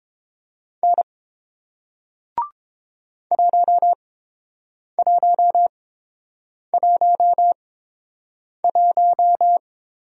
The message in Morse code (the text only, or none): N 1111